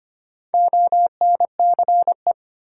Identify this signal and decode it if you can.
Morse code, heard as a keyed tone, about 25 wpm, 700 Hz. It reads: ONCE
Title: Sentences from Top 300 Words - New Word Drill - 25wpm